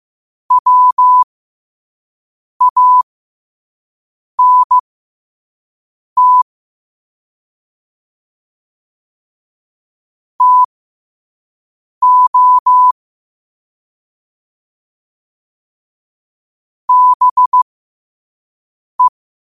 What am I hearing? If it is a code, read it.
Morse code, heard as a keyed tone, about 15 wpm, spaced as for 5 wpm, 1000 Hz. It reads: WANT TO BE
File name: 5wpm